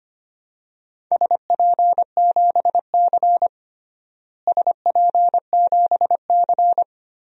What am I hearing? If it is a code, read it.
Morse code, heard as a keyed tone, about 25 wpm, 700 Hz. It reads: SP7C SP7C
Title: DXCC-100 Call Signs - 4 Characters - Repeated 3x - 25wpm